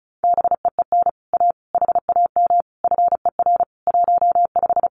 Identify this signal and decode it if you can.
Morse code, heard as a keyed tone, about 36 wpm, 700 Hz. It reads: BEEN A HAM FER 15
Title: Common QSO Phrases Encore 35wpm